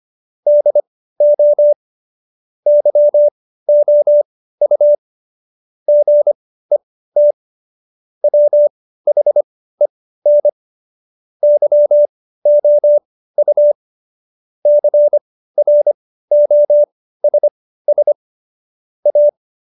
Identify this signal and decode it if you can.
Morse code, heard as a keyed tone, about 25 wpm, spaced as for 15 wpm, 590 Hz. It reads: DO YOU GET WHEN YOU CROSS A
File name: Jokes-25-char-15-eff